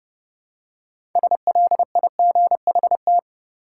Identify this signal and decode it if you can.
Morse code, heard as a keyed tone, about 30 wpm, 700 Hz. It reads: SLIGHT
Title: Six Letter Words - 30wpm